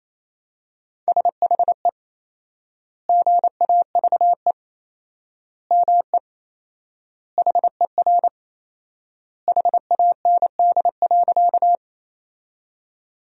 Morse code, heard as a keyed tone, about 28 wpm, 700 Hz. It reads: SHE GAVE ME HER HAND.
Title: 4x Word-Spacing - Sentences from Top 500 Words - 28wpm